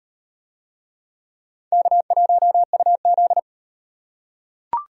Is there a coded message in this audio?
Yes: Morse code from 1.7 to 3.4 s, heard as a keyed tone, about 37 wpm, 700 Hz. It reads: K1UZ